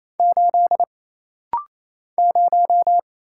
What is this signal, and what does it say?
Morse code, heard as a keyed tone, about 28 wpm, 700 Hz. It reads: 8 0
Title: Single Letter-Number - Rapid-Fire 28wpm